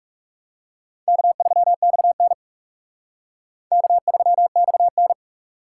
Morse code, heard as a keyed tone, about 37 wpm, 700 Hz. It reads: K3XN K3XN
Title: Speed Racing - USA 1x2 Call Signs 30wpm